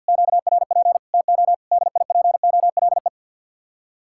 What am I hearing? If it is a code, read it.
Morse code, heard as a keyed tone, about 65 wpm, spaced as for 57 wpm, 700 Hz. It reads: QRP TO DIPOLE
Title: Speed-Racing Common QSO Phrases - 50wpm